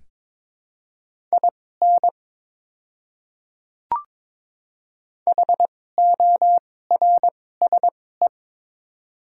Morse code, heard as a keyed tone, about 22 wpm, spaced as for 16 wpm, 700 Hz. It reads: IN HORSE
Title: 2x Character Spacing - Top 300 Words - 22wpm